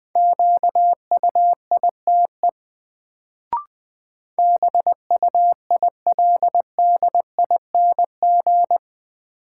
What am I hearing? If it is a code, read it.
Morse code, heard as a keyed tone, about 20 wpm, 700 Hz. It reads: QUITE BUILDING
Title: Sentences from Top 700 Words - New Word Drill - 20wpm